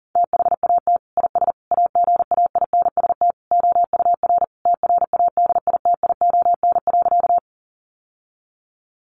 Morse code, heard as a keyed tone, about 40 wpm, 700 Hz. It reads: THAT IS AGAINST OUR TRADITION.
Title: Sentences from Top 1500 Words 40wpm